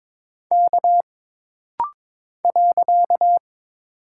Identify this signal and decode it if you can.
Morse code, heard as a keyed tone, about 22 wpm, 700 Hz. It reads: K .
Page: This rapid-fire timing is geared to push your ICR (Instant Character Recognition) skills.